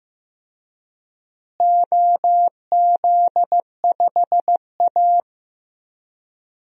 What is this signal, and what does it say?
Morse code, heard as a keyed tone, about 15 wpm, 700 Hz. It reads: OZ5A
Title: DXCC-100 Call Signs - 4 Characters - Repeated 3x - 15wpm